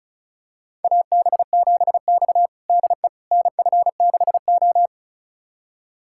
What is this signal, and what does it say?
Morse code, heard as a keyed tone, about 34 wpm, 700 Hz. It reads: AB7X DE NF6O
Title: USA Callsign DE Callsign - 4 Characters - 35wpm